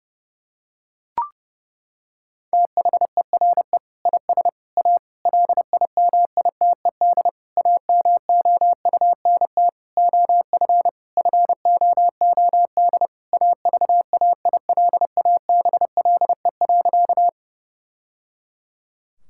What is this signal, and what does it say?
Morse code, heard as a keyed tone, about 30 wpm, 700 Hz. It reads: THERE IS A LIMITED AMOUNT OF FOOD AVAILABLE.
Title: Sentences from Top 1000 Words - Review Part 6 - 30wpm